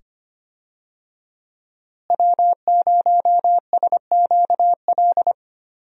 Morse code, heard as a keyed tone, about 25 wpm, 700 Hz. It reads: W0SQL
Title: USA Call Signs - 5 Characters - Repeated 2x - 25wpm